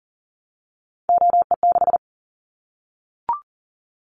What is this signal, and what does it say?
Morse code, heard as a keyed tone, about 40 wpm, 700 Hz. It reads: OE6
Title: Callsign Prefixes for Top 25 DXCC - Repeated 3x - 40wpm